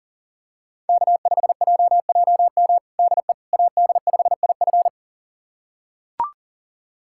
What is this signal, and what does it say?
Morse code, heard as a keyed tone, about 40 wpm, 700 Hz. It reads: K5JJM DE AD5IF
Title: USA Callsign DE Callsign - 5 Characters - 40wpm